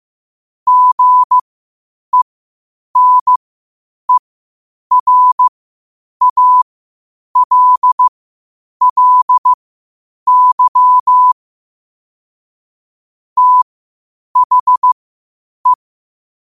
Morse code, heard as a keyed tone, about 15 wpm, spaced as for 8 wpm, 1000 Hz. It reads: GENERALLY THE
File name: qotdfiles / 8wpm